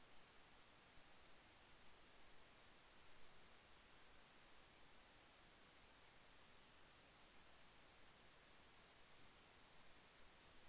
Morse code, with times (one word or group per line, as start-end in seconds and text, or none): none